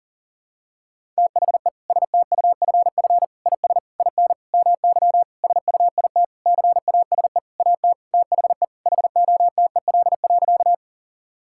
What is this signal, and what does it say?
Morse code, heard as a keyed tone, about 40 wpm, 700 Hz. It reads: THE STUFF IS IN MY SUIT CASE AT THE HOTEL.
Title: Sentences from Top 1600 Words 40wpm